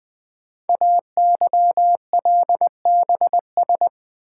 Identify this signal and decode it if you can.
Morse code, heard as a keyed tone, about 20 wpm, 700 Hz. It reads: AYLBS
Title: Code Groups - 5 Letter:Numbers - 20wpm